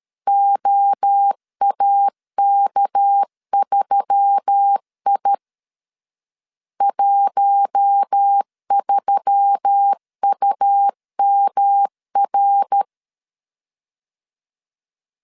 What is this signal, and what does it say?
Morse code, heard as a keyed tone, about 13 wpm, 800 Hz. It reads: OAK3I 13UMR
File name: CW / DET / 1 6 1500C S B 12wpm 3431